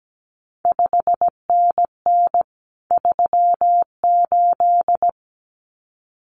Morse code, heard as a keyed tone, about 17 wpm, 700 Hz. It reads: 5NN 38
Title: CQ WW DX Contest Exchange Encore - 17wpm